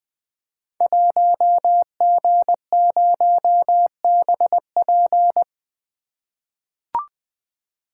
Morse code, heard as a keyed tone, about 21 wpm, 700 Hz. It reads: 1G0BP
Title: Code Groups - 5 Letter:Numbers - 20wpm